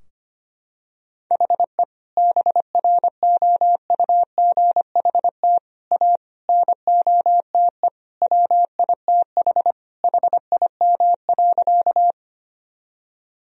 Morse code, heard as a keyed tone, about 25 wpm, 700 Hz. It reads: HE BROUGHT A NOTE WITH HIM.
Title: Sentences from Top 1000 Words - Review Part 2 - 25wpm